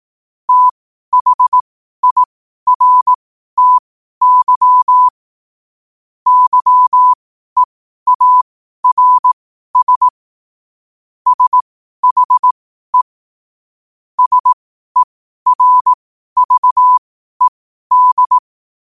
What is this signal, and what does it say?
Morse code, heard as a keyed tone, about 18 wpm, spaced as for 12 wpm, 1000 Hz. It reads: THIRTY YEARS SHE SERVED